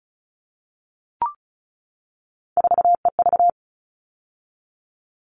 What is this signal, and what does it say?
Morse code, heard as a keyed tone, about 34 wpm, 700 Hz. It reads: 4EV
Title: All Cut Numbers - 3-Digit 35wpm